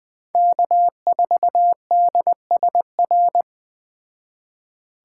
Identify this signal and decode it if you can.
Morse code, heard as a keyed tone, about 20 wpm, 700 Hz. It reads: K4DSR
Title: Speed Racing - USA 1x3 Call Signs 17wpm